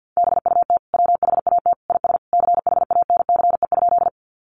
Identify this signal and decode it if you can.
Morse code, heard as a keyed tone, about 50 wpm, 700 Hz. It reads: BUT WHAT IS CHANCE?
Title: Sentences from Top 800 Words 50wpm